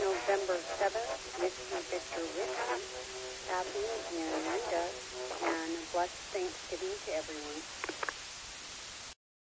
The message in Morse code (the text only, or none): none